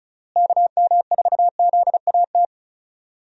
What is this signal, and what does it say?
Morse code, heard as a keyed tone, about 36 wpm, 700 Hz. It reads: KM4ZAT
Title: USA Call Signs - 6 Characters - Repeated 3x - 35wpm